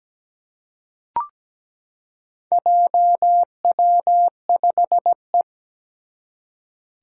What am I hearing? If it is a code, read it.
Morse code, heard as a keyed tone, about 17 wpm, 700 Hz. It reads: JW5E
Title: Four Challenging Letter-Number Combinations 17wpm